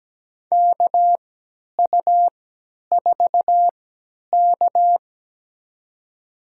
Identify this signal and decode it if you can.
Morse code, heard as a keyed tone, about 17 wpm, spaced as for 10 wpm, 700 Hz. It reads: KU4K
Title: USA Call Signs - 4 Characters - 3x Character Spacing - 17wpm